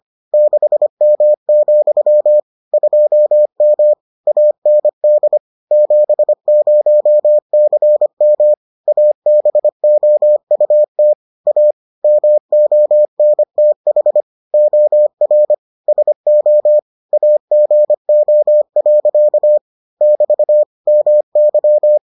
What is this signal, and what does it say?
Morse code, heard as a keyed tone, about 25 wpm, 600 Hz. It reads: 6M, 2M AND 70CM ABOUT A MONTH OR SO AGO. <BT> MY